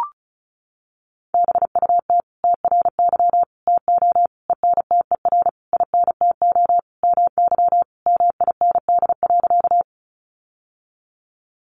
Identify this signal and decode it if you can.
Morse code, heard as a keyed tone, about 34 wpm, 700 Hz. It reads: BUT TRY TO ENTER INTO MY MIND.
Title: Sentences from Top 1300 Words 35wpm